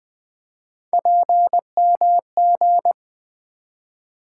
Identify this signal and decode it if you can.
Morse code, heard as a keyed tone, about 20 wpm, 700 Hz. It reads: PMG
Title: Three Letter-Number Combinations - 20wpm